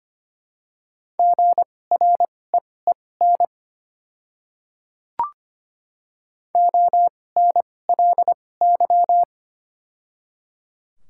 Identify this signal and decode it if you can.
Morse code, heard as a keyed tone, about 25 wpm, spaced as for 18 wpm, 700 Hz. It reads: GREEN ONLY
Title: 2x Character Spacing - Top 500 Words - 25wpm